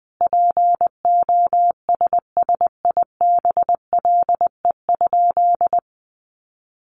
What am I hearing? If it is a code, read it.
Morse code, heard as a keyed tone, about 20 wpm, 700 Hz. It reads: POSSIBLE?
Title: Sentences from Top 500 Words 20wpm